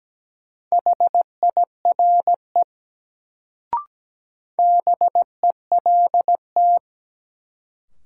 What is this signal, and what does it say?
Morse code, heard as a keyed tone, about 17 wpm, 700 Hz. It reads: HIRE BELT
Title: Sentences from Top 1400 Words - New Word Drill Alt - 17wpm